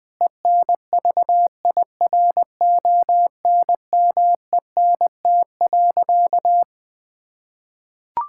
Morse code, heard as a keyed tone, about 20 wpm, 700 Hz. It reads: ENVIRONMENT.
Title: Sentences from Top 1000 Words - Review Part 5 - 20wpm